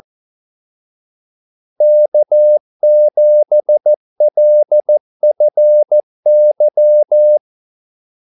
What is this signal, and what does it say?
Morse code, heard as a keyed tone, about 14 wpm, 600 Hz. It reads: K7LFY
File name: Callsigns-14wpm